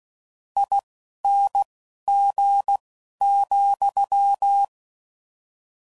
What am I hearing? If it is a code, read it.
Morse code, heard as a keyed tone, about 16 wpm, spaced as for 12 wpm, 800 Hz. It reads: ING,
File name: Mixed / Variable Sp / T AOEIs